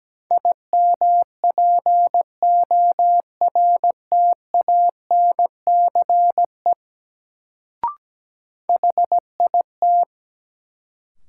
Morse code, heard as a keyed tone, about 17 wpm, 700 Hz. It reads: IMPORTANCE HIT
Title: Sentences from Top 1200 Words - New Word Drill - 17wpm